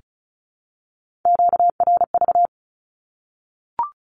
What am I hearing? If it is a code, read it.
Morse code, heard as a keyed tone, about 34 wpm, 700 Hz. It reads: QRV
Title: Q-Codes Repeated 3x 35wpm